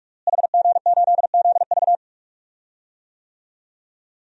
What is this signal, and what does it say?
Morse code, heard as a keyed tone, about 46 wpm, 700 Hz. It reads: HG8ZV
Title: DXCC-100 Call Signs - 5 Characters - Repeated 3x - 45wpm